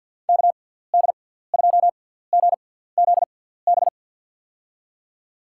Morse code, heard as a keyed tone, about 53 wpm, spaced as for 17 wpm, 700 Hz. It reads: KD2GZB
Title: USA Call Signs - 6 Characters - 6x Character Spacing - 50wpm